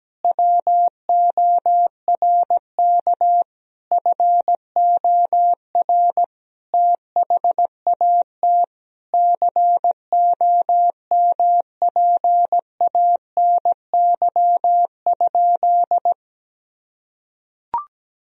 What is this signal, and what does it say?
Morse code, heard as a keyed tone, about 17 wpm, 700 Hz. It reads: WORK FOR THAT COMPANY?
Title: Sentences from Top 1000 Words - Review Part 3 - 17wpm